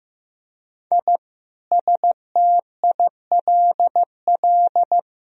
Morse code, heard as a keyed tone, about 15 wpm, 700 Hz. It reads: I STILL
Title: Sentences from Top 1000 Words - Review Part 1 - 15wpm